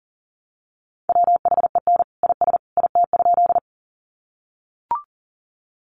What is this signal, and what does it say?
Morse code, heard as a keyed tone, about 40 wpm, 700 Hz. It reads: WHEN IS IT?
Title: Sentences from Top 100 Words 40wpm